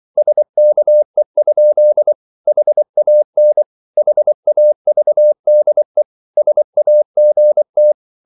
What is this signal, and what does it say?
Morse code, heard as a keyed tone, about 24 wpm, 600 Hz. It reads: SKE? HAN HAVDE SAGT
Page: Ferie 24 wpm